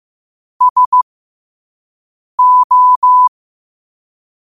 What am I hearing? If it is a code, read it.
Morse code, heard as a keyed tone, about 15 wpm, spaced as for 5 wpm, 1000 Hz. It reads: SO
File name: qotdfiles / 5wpm